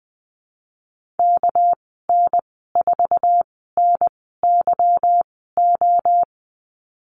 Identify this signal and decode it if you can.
Morse code, heard as a keyed tone, about 20 wpm, spaced as for 14 wpm, 700 Hz. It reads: KN4NYO
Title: 200 Lesson 32g - Letter k - Call Signs 20wpm